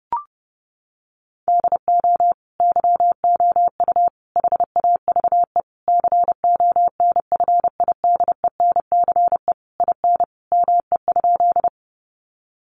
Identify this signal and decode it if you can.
Morse code, heard as a keyed tone, about 28 wpm, 700 Hz. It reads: DO YOU HAVE CONFIDENCE IN ME?
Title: Sentences from Top 1500 Words 30wpm